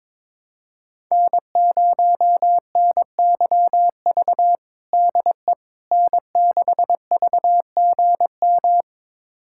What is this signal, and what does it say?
Morse code, heard as a keyed tone, about 22 wpm, 700 Hz. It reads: N0NYV DE N6VGM
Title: USA Callsign DE Callsign - 5 Characters - 22wpm